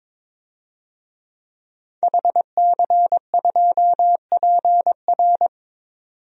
Morse code, heard as a keyed tone, about 22 wpm, 700 Hz. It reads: HC2PR
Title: DXCC-100 Call Signs - 5 Characters - 22wpm